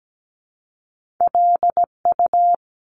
Morse code, heard as a keyed tone, about 17 wpm, 700 Hz. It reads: LU
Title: Top Two Letter Combinations in Words 17wpm